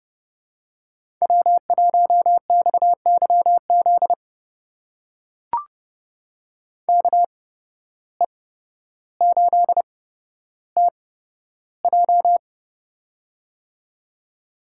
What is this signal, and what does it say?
Morse code, heard as a keyed tone, about 30 wpm, spaced as for 13 wpm, 700 Hz. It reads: W1XYZ KE8TJ
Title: USA Call Signs - 5 Characters - 8x Character Spacing - 30wpm